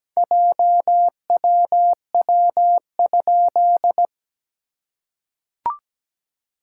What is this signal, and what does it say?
Morse code, heard as a keyed tone, about 17 wpm, 700 Hz. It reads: JWW?
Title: SOTA QSO Practice 17wpm